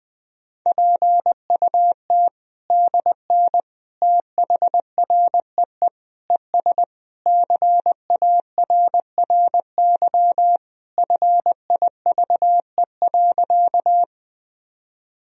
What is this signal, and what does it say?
Morse code, heard as a keyed tone, about 20 wpm, 700 Hz. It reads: PUT DN THREE ES CARRY FIVE.
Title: Sentences from Top 400 Words and Abbreviations 20wpm